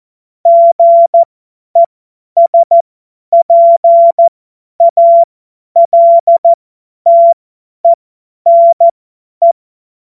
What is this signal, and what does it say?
Morse code, heard as a keyed tone, about 14 wpm, spaced as for 10 wpm, 680 Hz. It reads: GESPALTENE
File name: Morsekurs / Witze / Witz 25. Oktober 2022 50 F70